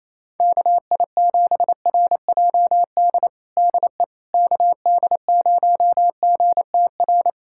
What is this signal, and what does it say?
Morse code, heard as a keyed tone, about 28 wpm, 700 Hz. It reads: KI7RJD DE KD0GTR
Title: USA Callsign DE Callsign - 6 Characters - 28wpm